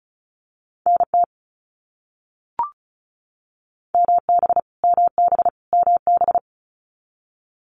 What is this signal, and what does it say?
Morse code, heard as a keyed tone, about 34 wpm, 700 Hz. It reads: NT MB MB MB
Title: Canadian Provinces and Territories - Repeated 3x - 35wpm